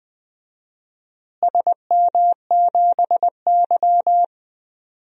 Morse code, heard as a keyed tone, about 20 wpm, 700 Hz. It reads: SM7Y
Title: DXCC-100 Call Signs - 4 Characters - Repeated 2x - 20wpm